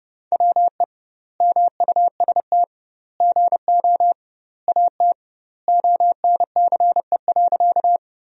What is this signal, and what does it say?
Morse code, heard as a keyed tone, about 30 wpm, 700 Hz. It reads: WE MUST GO AT ONCE.
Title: 2x Word-Spacing - Sentences from Top 300 Words - 30wpm